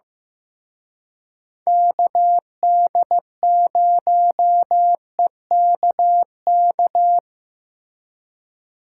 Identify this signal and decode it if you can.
Morse code, heard as a keyed tone, about 17 wpm, 700 Hz. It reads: KD0EKK
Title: Speed Racing - USA Call Signs 15wpm